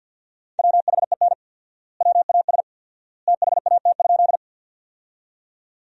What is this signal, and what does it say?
Morse code, heard as a keyed tone, about 50 wpm, 700 Hz. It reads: WHEN WAS THAT?
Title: 4x Word-Spacing - Sentences from Top 100 Words - 50wpm